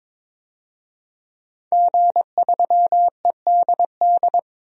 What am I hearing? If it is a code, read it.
Morse code, heard as a keyed tone, about 21 wpm, 700 Hz. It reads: G3EDD
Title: DXCC-100 Call Signs - 5 Characters - Repeated 3x - 22wpm